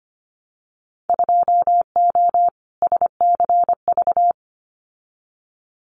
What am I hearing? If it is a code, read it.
Morse code, heard as a keyed tone, about 25 wpm, 700 Hz. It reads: 2O SCV
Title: Winter Field Day Exchanges Encore 25wpm